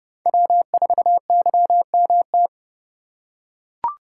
Morse code, heard as a keyed tone, about 31 wpm, 700 Hz. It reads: W4YMT
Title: USA Call Signs - 5 Characters - 3x Character Spacing - 30wpm